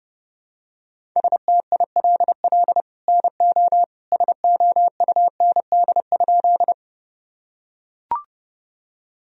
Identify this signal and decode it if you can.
Morse code, heard as a keyed tone, about 30 wpm, 700 Hz. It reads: STILL NO SOUND?
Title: Sentences from Top 1000 Words - Review Part 1 - 30wpm